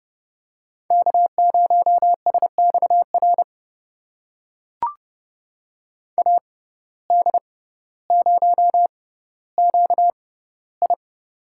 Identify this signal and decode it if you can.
Morse code, heard as a keyed tone, about 30 wpm, spaced as for 15 wpm, 700 Hz. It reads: K0SXR AD0QI
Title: USA Call Signs - 5 Characters - 6x Character Spacing - 30wpm